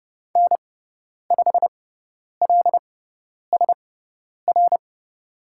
Morse code, heard as a keyed tone, about 30 wpm, spaced as for 10 wpm, 700 Hz. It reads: N5LSR
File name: Lesson 16g - Letter c - Call Signs 30wpm